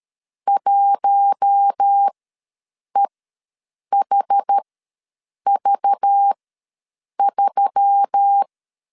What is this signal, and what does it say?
Morse code, heard as a keyed tone, about 13 wpm, spaced as for 7 wpm, 800 Hz. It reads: 1EHV3
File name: LEZIONI / 6 PKX0 / 6 PK ws 12wpm 3116